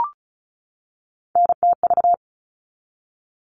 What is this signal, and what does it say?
Morse code, heard as a keyed tone, about 34 wpm, 700 Hz. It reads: NTV
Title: All Cut Numbers - 3-Digit 35wpm